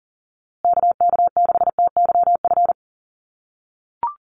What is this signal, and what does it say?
Morse code, heard as a keyed tone, about 40 wpm, 700 Hz. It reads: KK6TYF
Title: USA Call Signs Repeated 2x 40wpm